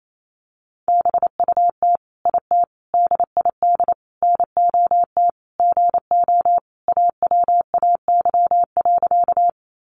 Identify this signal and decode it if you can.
Morse code, heard as a keyed tone, about 28 wpm, 700 Hz. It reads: BUT IT DID NOT GO AWAY.
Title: Sentences from Top 600 Words 28wpm